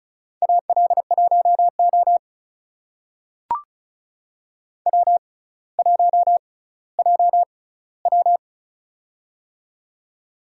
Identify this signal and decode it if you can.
Morse code, heard as a keyed tone, about 36 wpm, spaced as for 18 wpm, 700 Hz. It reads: AL1O W1JW
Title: USA Call Signs - 4 Characters - 6x Character Spacing - 35wpm